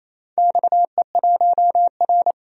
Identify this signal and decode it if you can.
Morse code, heard as a keyed tone, about 27 wpm, 700 Hz. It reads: XE1R
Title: DXCC-100 Call Signs - 4 Characters - 28wpm